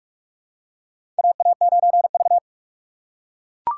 Morse code, heard as a keyed tone, about 44 wpm, 690 Hz. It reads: AA9V
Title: USA Call Signs - 4 Characters - 3x Character Spacing - 45wpm